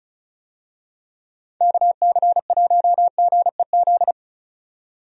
Morse code, heard as a keyed tone, about 36 wpm, 700 Hz. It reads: KC1GEZ
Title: USA Call Signs - 6 Characters - 35wpm